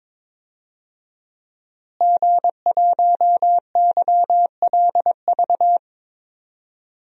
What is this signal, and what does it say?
Morse code, heard as a keyed tone, about 22 wpm, 700 Hz. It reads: G1YLV
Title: DXCC-100 Call Signs - 5 Characters - Repeated 3x - 22wpm